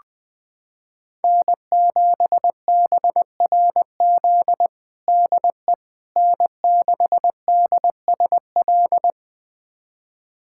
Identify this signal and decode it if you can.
Morse code, heard as a keyed tone, about 20 wpm, 700 Hz. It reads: N7BRZ DE N6DSL
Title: USA Callsign DE Callsign - 5 Characters - 20wpm